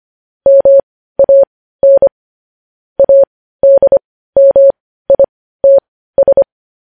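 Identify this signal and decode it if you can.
Morse code, heard as a keyed tone, about 25 wpm, spaced as for 15 wpm, 550 Hz. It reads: MAN ADMITS
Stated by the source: Morse code news headlines broadcast.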